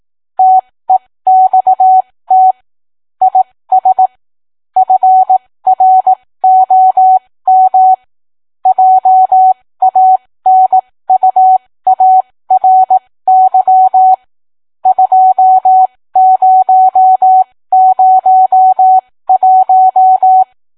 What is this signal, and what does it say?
Morse code, heard as a keyed tone, about 18 wpm, spaced as for 15 wpm, 750 Hz. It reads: TEXT IS FROM JANUARY 2001